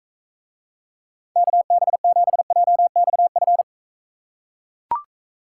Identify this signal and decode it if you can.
Morse code, heard as a keyed tone, about 44 wpm, 700 Hz. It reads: KB7JXF